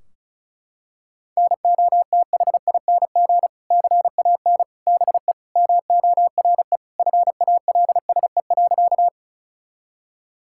Morse code, heard as a keyed tone, about 36 wpm, 700 Hz. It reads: NOTHING CAN BE MORE FALSE.
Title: Sentences from Top 1600 Words 35wpm